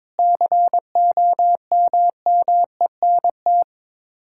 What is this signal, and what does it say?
Morse code, heard as a keyed tone, about 22 wpm, 700 Hz. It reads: COMMENT
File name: practice / 22wpm / Sets of 8 Words - 22wpm